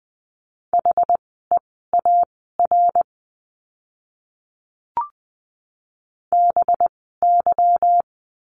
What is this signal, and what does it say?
Morse code, heard as a keyed tone, about 20 wpm, spaced as for 14 wpm, 700 Hz. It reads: HEAR BY
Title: 230 Lesson 37f - Letter x - Words 20wpm